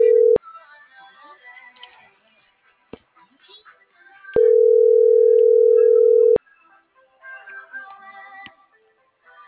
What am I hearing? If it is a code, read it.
Ringback tone (North America), 440 + 480 Hz.